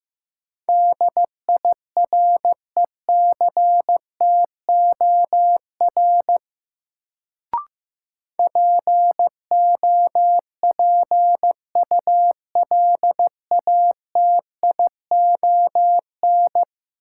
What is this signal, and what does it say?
Morse code, heard as a keyed tone, about 15 wpm, 700 Hz. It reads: DIRECTOR POPULATION
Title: Sentences from Top 1300 Words - New Word Drill Alt - 15wpm